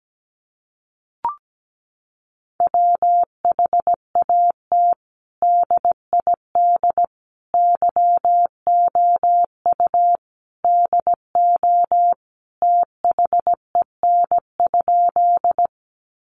Morse code, heard as a keyed tone, about 17 wpm, 700 Hz. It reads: WHAT DID YOU DO THEN?
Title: Sentences from Top 100 Words 17wpm